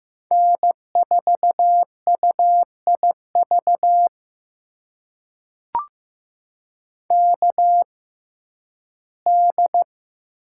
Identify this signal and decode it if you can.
Morse code, heard as a keyed tone, about 15 wpm, spaced as for 11 wpm, 700 Hz. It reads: N4UIV KD
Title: USA Call Signs - 5 Characters - 6x Character Spacing - 15wpm